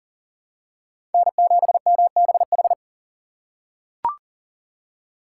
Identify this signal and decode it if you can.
Morse code, heard as a keyed tone, about 40 wpm, 700 Hz. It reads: N7MBH
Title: USA Call Signs - 5 Characters - 4x Character Spacing - 40wpm